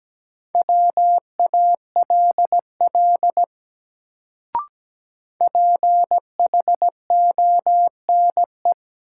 Morse code, heard as a keyed tone, about 17 wpm, 700 Hz. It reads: WALL PHONE
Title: Sentences from Top 1100 Words - New Word Drill Alt - 17wpm